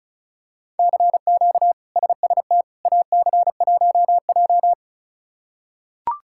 Morse code, heard as a keyed tone, about 34 wpm, 700 Hz. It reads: CQ SST AC1J
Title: K1USN SST QSO Practice - 35wpm